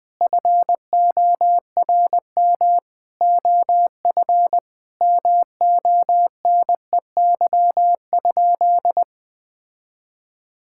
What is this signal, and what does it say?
Morse code, heard as a keyed tone, about 21 wpm, 700 Hz. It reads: FORM OF MONEY?
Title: Sentences from Top 1000 Words - Review Part 6 - 20wpm